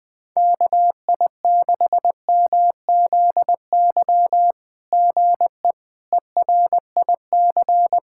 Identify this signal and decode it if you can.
Morse code, heard as a keyed tone, about 20 wpm, 700 Hz. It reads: KI6MZY GE ERIC
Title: SOTA QSO Practice 20wpm